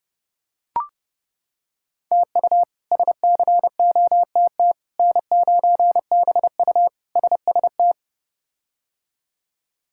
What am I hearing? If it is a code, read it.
Morse code, heard as a keyed tone, about 32 wpm, 700 Hz. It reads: TU SCOTT N9BU SST
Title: K1USN SST QSO Practice - 30wpm